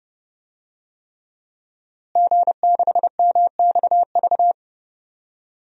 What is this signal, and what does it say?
Morse code, heard as a keyed tone, about 30 wpm, 700 Hz. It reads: G6MXV
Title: DXCC-100 Call Signs - 5 Characters - Repeated 3x - 30wpm